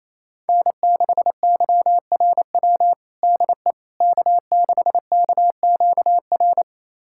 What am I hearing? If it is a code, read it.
Morse code, heard as a keyed tone, about 28 wpm, 700 Hz. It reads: N6YRW DE K6KQR
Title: USA Callsign DE Callsign - 5 Characters - 28wpm